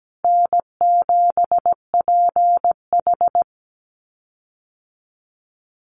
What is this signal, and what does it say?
Morse code, heard as a keyed tone, about 17 wpm, 700 Hz. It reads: N7PH
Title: USA Call Signs Repeated 3x 17wpm